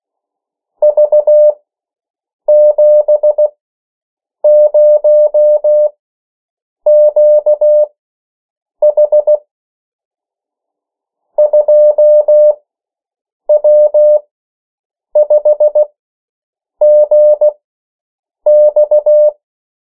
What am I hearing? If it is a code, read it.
Morse code, heard as a keyed tone, about 16 wpm, spaced as for 7 wpm, 600 Hz. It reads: V70QH 2W5GX
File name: morse mp3 / 14 urok 13